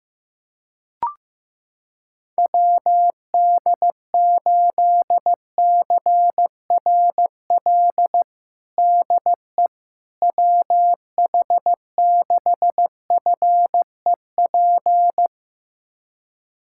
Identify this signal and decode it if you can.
Morse code, heard as a keyed tone, about 16 wpm, 700 Hz. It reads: WD8CRL DE WH6FEP
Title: USA Callsign DE Callsign - 6 Characters - 15wpm